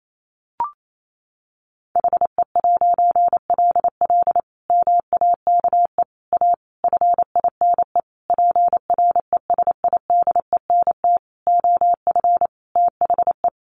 Morse code, heard as a keyed tone, about 28 wpm, 700 Hz. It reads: HE'LL MAKE A FINE PRESIDENT OF THE
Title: A Sound Of Thunder 28wpm